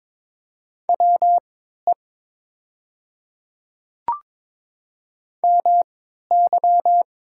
Morse code, heard as a keyed tone, about 22 wpm, spaced as for 13 wpm, 700 Hz. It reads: WE MY
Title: 3x Character Spacing - Top 100 Words - 22wpm